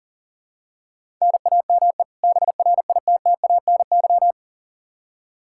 Morse code, heard as a keyed tone, about 40 wpm, 700 Hz. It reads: NAME BRITTANY
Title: K1USN SST QSO Practice 40wpm